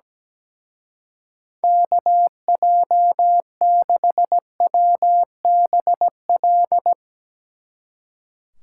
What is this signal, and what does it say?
Morse code, heard as a keyed tone, about 17 wpm, 700 Hz. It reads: KJ6WBL
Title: Speed Racing - USA Call Signs 17wpm